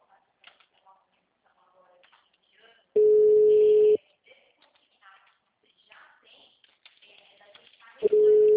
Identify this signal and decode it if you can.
Ringback tone (Europe, 425 Hz).